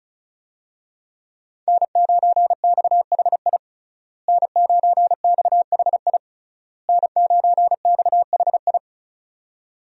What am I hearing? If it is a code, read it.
Morse code, heard as a keyed tone, about 34 wpm, 700 Hz. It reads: N9XHI N9XHI N9XHI
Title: USA Call Signs - 5 Characters - Repeated 3x - 35wpm